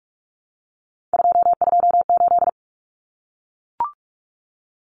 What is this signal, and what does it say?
Morse code, heard as a keyed tone, about 44 wpm, 700 Hz. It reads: 228
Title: Numbers - 3-Digit 45wpm